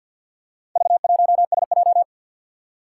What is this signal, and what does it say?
Morse code, heard as a keyed tone, about 52 wpm, 700 Hz. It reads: V1SJ
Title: Four Challenging Letter-Number Combinations 50wpm